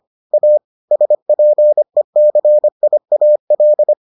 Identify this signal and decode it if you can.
Morse code, heard as a keyed tone, about 25 wpm, 600 Hz. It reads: A SPECIAL